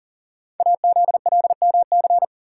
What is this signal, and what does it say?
Morse code, heard as a keyed tone, about 38 wpm, 690 Hz. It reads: AZLMC
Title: Code Groups - 5 Letters - 40wpm